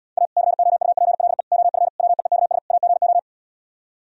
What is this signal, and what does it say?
Morse code, heard as a keyed tone, about 77 wpm, 700 Hz. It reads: A COUPLE OF WEEKS AGO